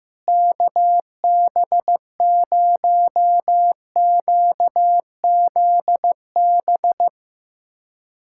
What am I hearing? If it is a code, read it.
Morse code, heard as a keyed tone, about 15 wpm, 700 Hz. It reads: KB0QZB
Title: USA Callsign DE Callsign 15wpm